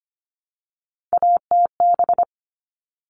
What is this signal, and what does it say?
Morse code, heard as a keyed tone, about 25 wpm, 700 Hz. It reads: ATB
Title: All Cut Numbers - 3-Digit 25wpm